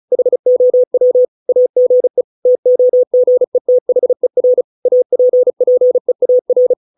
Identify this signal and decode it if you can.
Morse code, heard as a keyed tone, about 34 wpm, 500 Hz. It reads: HOW AGE TOGETHER APPEAR